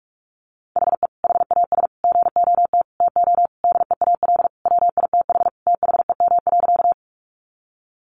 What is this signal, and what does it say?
Morse code, heard as a keyed tone, about 44 wpm, 700 Hz. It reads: HE HAS GOT TO DEAL WITH THEM.
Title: Sentences from Top 800 Words 45wpm